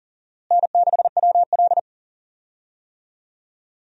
Morse code, heard as a keyed tone, about 40 wpm, 700 Hz. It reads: N6WL
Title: USA Call Signs - 4 Characters - Repeated 3x - 40wpm